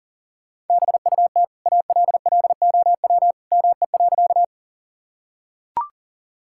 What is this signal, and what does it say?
Morse code, heard as a keyed tone, about 40 wpm, 700 Hz. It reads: BUT ALLOW ME.
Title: Sentences from Top 1000 Words - Review Part 5 - 40wpm